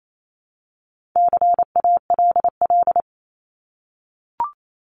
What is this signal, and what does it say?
Morse code, heard as a keyed tone, about 28 wpm, 700 Hz. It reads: CALL
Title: Top 300 Words 28wpm